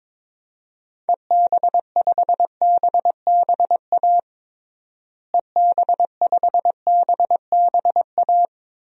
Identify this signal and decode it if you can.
Morse code, heard as a keyed tone, about 22 wpm, 700 Hz. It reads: EB5BBA EB5BBA
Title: DXCC-100 Call Signs - 6 Characters - Repeated 3x - 22wpm